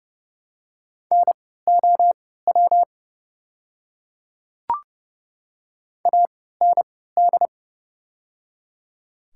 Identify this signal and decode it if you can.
Morse code, heard as a keyed tone, about 30 wpm, spaced as for 17 wpm, 700 Hz. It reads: NOW AND
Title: 3x Character Spacing - Top 100 Words - 30wpm